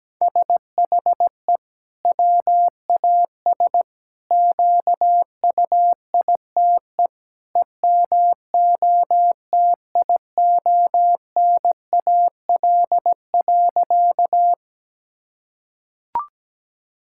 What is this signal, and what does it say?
Morse code, heard as a keyed tone, about 17 wpm, 700 Hz. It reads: SHE WAS QUITE EMOTIONAL.
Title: Sentences from Top 1000 Words - Review Part 5 - 17wpm